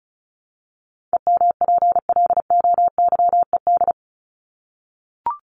Morse code, heard as a keyed tone, about 34 wpm, 700 Hz. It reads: EMPLOYED
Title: ED Suffix - Top 100 Words with Base Words 35wpm